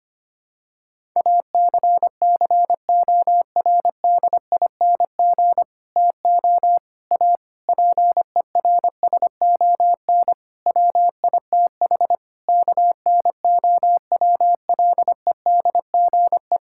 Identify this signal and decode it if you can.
Morse code, heard as a keyed tone, about 25 wpm, 700 Hz. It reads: ACCORDING TO A PERSON WITH KNOWLEDGE
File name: practice / 25wpm / Sentences with 9-Word Phrases - 25wpm